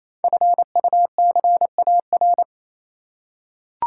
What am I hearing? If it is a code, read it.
Morse code, heard as a keyed tone, about 28 wpm, 700 Hz. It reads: FUCAR
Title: Code Groups - 5 Letter:Numbers - 28wpm